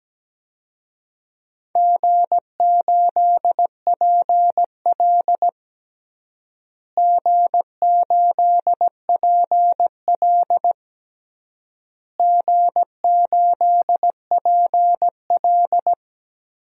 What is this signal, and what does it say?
Morse code, heard as a keyed tone, about 17 wpm, 700 Hz. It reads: G8PL G8PL G8PL
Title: DXCC-100 Call Signs - 4 Characters - Repeated 3x - 17wpm